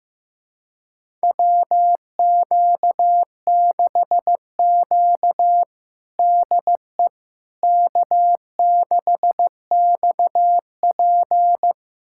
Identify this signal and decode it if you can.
Morse code, heard as a keyed tone, about 16 wpm, 700 Hz. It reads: WQ6Q DE K6XP
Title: USA Callsign DE Callsign - 4 Characters - 15wpm